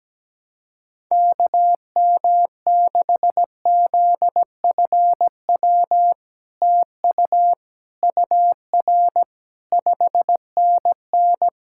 Morse code, heard as a keyed tone, about 17 wpm, 700 Hz. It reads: KM6ZFW TU UR 5NN
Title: SOTA QSO Practice 17wpm